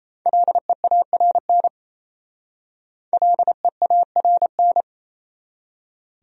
Morse code, heard as a keyed tone, about 36 wpm, 700 Hz. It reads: LEARN LEARN